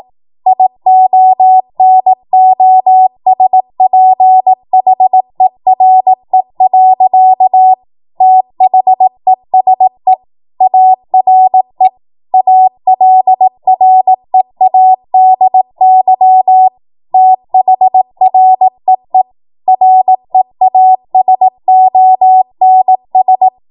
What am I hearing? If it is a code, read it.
Morse code, heard as a keyed tone, about 18 wpm, 750 Hz. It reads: IONOSPHERE. THESE ARE ALREADY THREE REASONS
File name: Morse / 18WPM